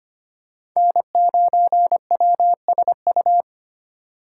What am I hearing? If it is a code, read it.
Morse code, heard as a keyed tone, about 25 wpm, 700 Hz. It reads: N9WSU
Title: USA Call Signs - 5 Characters - 4x Character Spacing - 25wpm